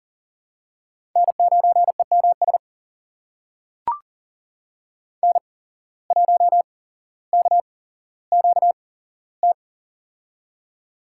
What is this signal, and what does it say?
Morse code, heard as a keyed tone, about 40 wpm, spaced as for 17 wpm, 700 Hz. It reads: N9EMS N1KQT
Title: USA Call Signs - 5 Characters - 8x Character Spacing - 40wpm